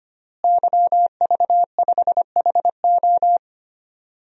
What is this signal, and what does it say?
Morse code, heard as a keyed tone, about 25 wpm, 700 Hz. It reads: YV5HO
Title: DXCC-100 Call Signs - 5 Characters - Repeated 2x - 25wpm